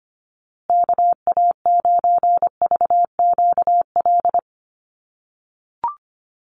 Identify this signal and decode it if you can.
Morse code, heard as a keyed tone, about 25 wpm, 700 Hz. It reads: KA9VQL
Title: USA Call Signs Repeated 3x 25wpm